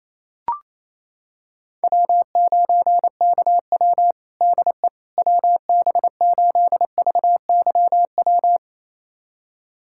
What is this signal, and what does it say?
Morse code, heard as a keyed tone, about 28 wpm, 700 Hz. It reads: W9KW DE WB8VYW
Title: USA Callsign DE Callsign 28wpm